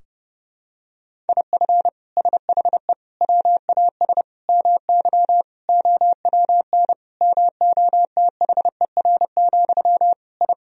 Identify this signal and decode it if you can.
Morse code, heard as a keyed tone, about 30 wpm, 700 Hz. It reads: IF SHE WAS MY OWN MOTHER, I
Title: Sentences from Top 1000 Words - Review Part 2 - 30wpm